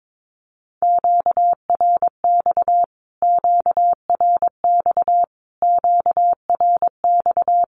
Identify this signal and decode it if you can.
Morse code, heard as a keyed tone, about 22 wpm, 700 Hz. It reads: QRX QRX QRX
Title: Q-Codes Repeated 3x 22wpm